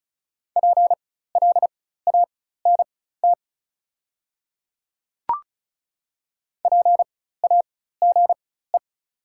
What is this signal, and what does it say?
Morse code, heard as a keyed tone, about 36 wpm, spaced as for 16 wpm, 700 Hz. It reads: PLANT PAGE
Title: 4x Character Spacing - Top 300 Words - 35wpm